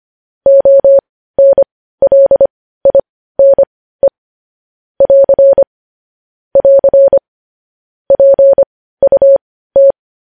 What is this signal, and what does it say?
Morse code, heard as a keyed tone, about 25 wpm, spaced as for 15 wpm, 550 Hz. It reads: ONLINE <AR> <AR> PUT
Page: news headlines in morse code at 15 wpm